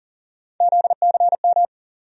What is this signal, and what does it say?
Morse code, heard as a keyed tone, about 40 wpm, 700 Hz. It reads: ZCM
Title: USA Call Sign Postfixes - 3 Letters 40wpm